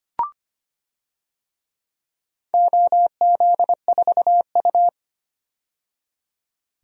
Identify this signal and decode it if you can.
Morse code, heard as a keyed tone, about 25 wpm, 700 Hz. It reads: OZ4U
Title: DXCC-100 Call Signs - 4 Characters - 25wpm